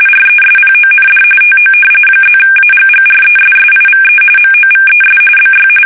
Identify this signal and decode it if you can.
RTTY, about 45.45 baud. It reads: RESPOND TO +AGGRESSIVE MOVES BY
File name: fsk850 / usnavy5